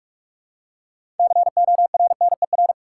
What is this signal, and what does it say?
Morse code, heard as a keyed tone, about 44 wpm, 700 Hz. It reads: CORNER
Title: Six Letter Words - 45wpm